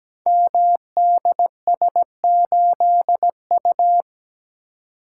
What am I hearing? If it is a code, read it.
Morse code, heard as a keyed tone, about 17 wpm, 700 Hz. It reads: MDS8U
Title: Code Groups - 5 Letter:Numbers - 17wpm